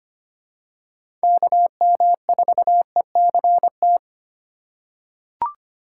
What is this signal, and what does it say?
Morse code, heard as a keyed tone, about 25 wpm, 700 Hz. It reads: KM4ECT
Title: USA Call Signs - 8x Character Spacing - 25wpm